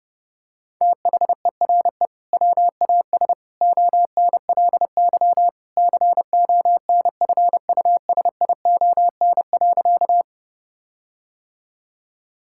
Morse code, heard as a keyed tone, about 32 wpm, 700 Hz. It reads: THERE WAS ONLY CONFUSION.
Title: Sentences from Top 1600 Words 30wpm